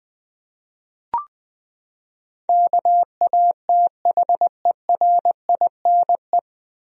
Morse code, heard as a keyed tone, about 20 wpm, 700 Hz. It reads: KATHERINE
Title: USA Common Names - 20wpm